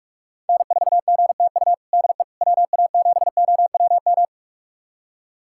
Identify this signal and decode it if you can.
Morse code, heard as a keyed tone, about 46 wpm, 700 Hz. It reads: N4GTU DE WA7OWM
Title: USA Callsign DE Callsign 45wpm